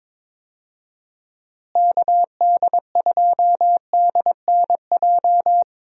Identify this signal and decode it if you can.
Morse code, heard as a keyed tone, about 22 wpm, 700 Hz. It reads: KD2DNJ
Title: USA Call Signs - 6 Characters - Repeated 2x - 22wpm